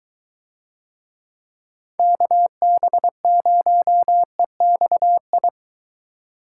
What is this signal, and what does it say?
Morse code, heard as a keyed tone, about 22 wpm, 700 Hz. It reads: KB0EXI